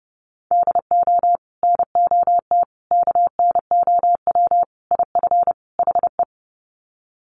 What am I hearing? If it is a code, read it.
Morse code, heard as a keyed tone, about 30 wpm, 700 Hz. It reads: DO NOT KNOW IF HE
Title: Sets of 5 Words Encore2 30wpm